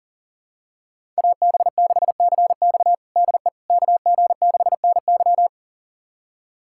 Morse code, heard as a keyed tone, about 40 wpm, 700 Hz. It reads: AB6CX DE KG6NY
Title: USA Callsign DE Callsign - 5 Characters - 40wpm